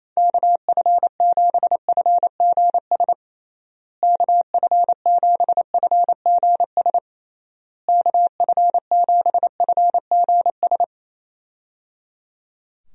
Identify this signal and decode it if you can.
Morse code, heard as a keyed tone, about 27 wpm, 700 Hz. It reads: KF7FGS KF7FGS KF7FGS
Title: USA Call Signs - 6 Characters - Repeated 3x - 28wpm